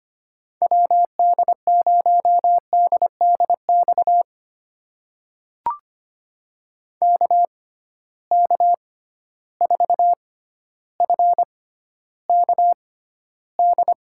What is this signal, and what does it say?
Morse code, heard as a keyed tone, about 25 wpm, spaced as for 13 wpm, 700 Hz. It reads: WD0DDX KK4FKD
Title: USA Call Signs - 6 Characters - 6x Character Spacing - 25wpm